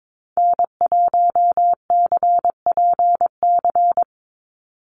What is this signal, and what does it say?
Morse code, heard as a keyed tone, about 22 wpm, 700 Hz. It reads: N1CPC
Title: USA Call Signs 22wpm